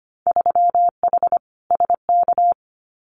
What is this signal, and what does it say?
Morse code, heard as a keyed tone, about 25 wpm, 700 Hz. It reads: 3H SK
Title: Winter Field Day Exchanges 25wpm